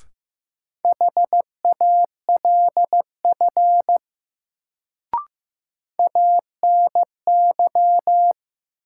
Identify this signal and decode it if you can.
Morse code, heard as a keyed tone, about 15 wpm, 700 Hz. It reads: HALF ANY
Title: Sets of 3 Words - New Word Drill - 15wpm